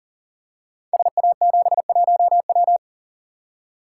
Morse code, heard as a keyed tone, about 43 wpm, 700 Hz. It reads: SA71W
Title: Code Groups - 5 Letter:Numbers - 40wpm